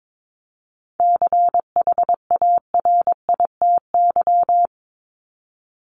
Morse code, heard as a keyed tone, about 22 wpm, 700 Hz. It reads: CHARITY
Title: Top 1501-1750 Words 22wpm